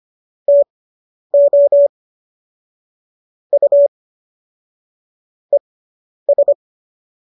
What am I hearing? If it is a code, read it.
Morse code, heard as a keyed tone, about 25 wpm, spaced as for 10 wpm, 570 Hz. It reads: TO U ES